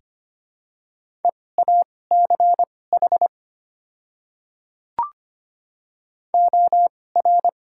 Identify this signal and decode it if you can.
Morse code, heard as a keyed tone, about 25 wpm, spaced as for 18 wpm, 700 Hz. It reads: EACH OR
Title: 2x Character Spacing - Top 100 Words - 25wpm